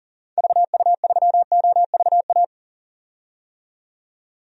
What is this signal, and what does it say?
Morse code, heard as a keyed tone, about 40 wpm, 700 Hz. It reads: VU3OVA
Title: DXCC-100 Call Signs - 6 Characters - 40wpm